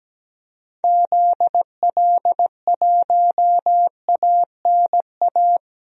Morse code, heard as a keyed tone, about 17 wpm, 700 Hz. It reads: ZL1ANA
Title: DXCC-100 Call Signs - 6 Characters - Repeated 3x - 17wpm